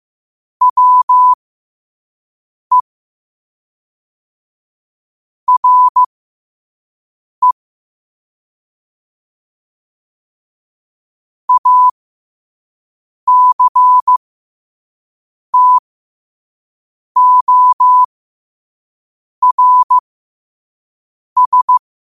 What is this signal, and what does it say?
Morse code, heard as a keyed tone, about 15 wpm, spaced as for 5 wpm, 1000 Hz. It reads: WE RE ACTORS
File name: qotdfiles / 5wpm